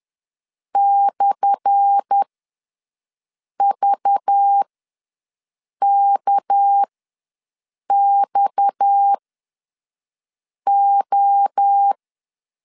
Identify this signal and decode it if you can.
Morse code, heard as a keyed tone, about 10 wpm, spaced as for 5 wpm, 800 Hz. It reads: /VKXO
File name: LEZIONI / 10 SIGNS / 10A SLASH ws 10wpm 4121